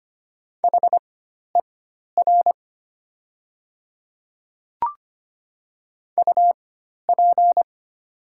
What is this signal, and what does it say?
Morse code, heard as a keyed tone, about 25 wpm, spaced as for 12 wpm, 700 Hz. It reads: HER UP
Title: 4x Character Spacing - Top 100 Words - 25wpm